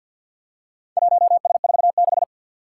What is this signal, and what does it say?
Morse code, heard as a keyed tone, about 45 wpm, 700 Hz. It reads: 1S46
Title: Four Challenging Letter-Number Combinations 50wpm